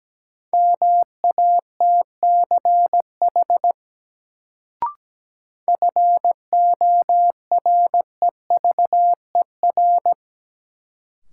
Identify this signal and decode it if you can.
Morse code, heard as a keyed tone, about 17 wpm, 700 Hz. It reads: MATCH FOREVER
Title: Sentences from Top 1600 Words - New Word Drill Alt - 17wpm